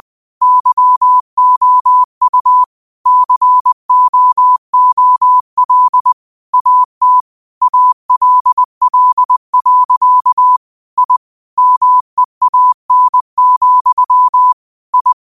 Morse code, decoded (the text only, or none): YOU COOL AT ALL. I MEAN, I